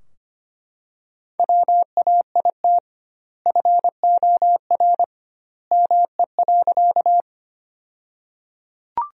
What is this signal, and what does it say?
Morse code, heard as a keyed tone, about 25 wpm, 700 Hz. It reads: WAIT FOR ME.
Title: 2x Word-Spacing - Sentences from Top 500 Words - 25wpm